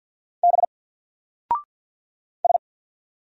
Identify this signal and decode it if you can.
Morse code, heard as a keyed tone, about 50 wpm, 700 Hz. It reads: B S
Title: Single Letter-Number 50wpm